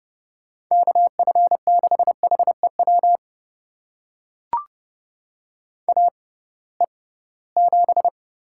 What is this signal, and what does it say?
Morse code, heard as a keyed tone, about 30 wpm, spaced as for 19 wpm, 700 Hz. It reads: KF6HEW AE7
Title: USA Call Signs - 6x Character Spacing - 30wpm